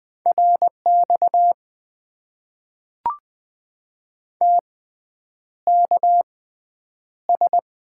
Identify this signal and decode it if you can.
Morse code, heard as a keyed tone, about 20 wpm, spaced as for 9 wpm, 700 Hz. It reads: RX TKS
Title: Top 90 QSO Elements - 6x Character Spacing - 20wpm